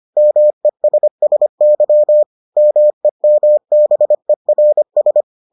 Morse code, heard as a keyed tone, about 25 wpm, 600 Hz. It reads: MESSY MEMBERS